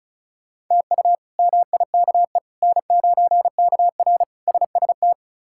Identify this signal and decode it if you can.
Morse code, heard as a keyed tone, about 34 wpm, 700 Hz. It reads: TU MIKE N9KR SST
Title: K1USN SST QSO Practice - 35wpm